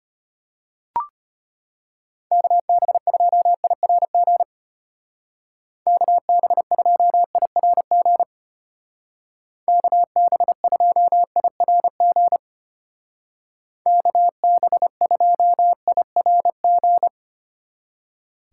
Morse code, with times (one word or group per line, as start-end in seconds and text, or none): 2.3-4.4 KB2IRG
5.9-8.2 KB2IRG
9.7-12.4 KB2IRG
13.9-17.1 KB2IRG